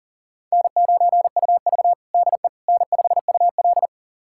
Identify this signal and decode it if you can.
Morse code, heard as a keyed tone, about 40 wpm, 700 Hz. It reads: N9UV DE N5UL
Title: USA Callsign DE Callsign - 4 Characters - 40wpm